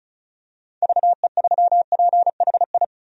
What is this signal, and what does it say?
Morse code, heard as a keyed tone, about 36 wpm, 700 Hz. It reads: VE3PHI
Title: DXCC-100 Call Signs - 6 Characters - 35wpm